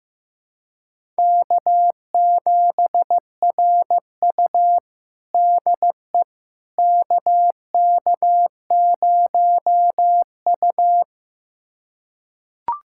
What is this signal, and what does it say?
Morse code, heard as a keyed tone, about 15 wpm, 700 Hz. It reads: K7RU DE KK0U
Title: USA Callsign DE Callsign - 4 Characters - 15wpm